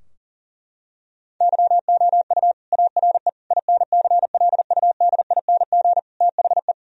Morse code, heard as a keyed tone, about 40 wpm, 700 Hz. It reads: YOU ARE INCLUDING THE
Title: Sentences from Top 1000 Words - Review Part 4 - 40wpm